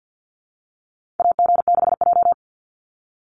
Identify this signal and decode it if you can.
Morse code, heard as a keyed tone, about 49 wpm, 700 Hz. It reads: AG6J
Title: USA Call Signs 50wpm